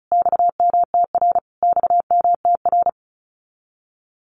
Morse code, heard as a keyed tone, about 34 wpm, 700 Hz. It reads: XMTR XMTR
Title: Abbreviations Repeated 3x - 35wpm